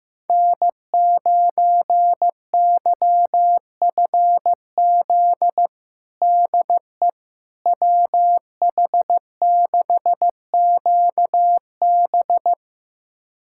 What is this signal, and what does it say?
Morse code, heard as a keyed tone, about 15 wpm, 700 Hz. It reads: N9YFZ DE WH6QB
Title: USA Callsign DE Callsign - 5 Characters - 15wpm